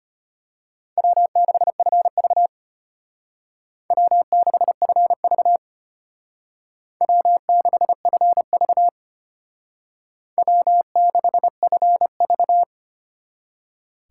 Morse code, heard as a keyed tone, about 32 wpm, 700 Hz. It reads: W6FV W6FV W6FV W6FV